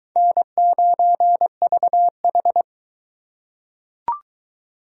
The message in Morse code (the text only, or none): N9VH